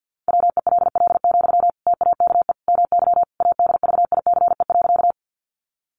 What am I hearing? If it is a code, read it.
Morse code, heard as a keyed tone, about 50 wpm, 700 Hz. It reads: WELL, TAKE MY ADVICE.
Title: Sentences from Top 1100 Words 50wpm